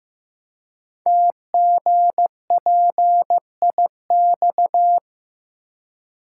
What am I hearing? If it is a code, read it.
Morse code, heard as a keyed tone, about 15 wpm, 700 Hz. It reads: TGPIX
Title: Code Groups - 5 Letters - 15wpm